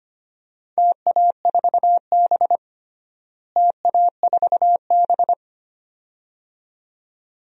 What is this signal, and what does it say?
Morse code, heard as a keyed tone, about 25 wpm, 700 Hz. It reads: TA4B TA4B
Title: DXCC-100 Call Signs - 4 Characters - Repeated 2x - 25wpm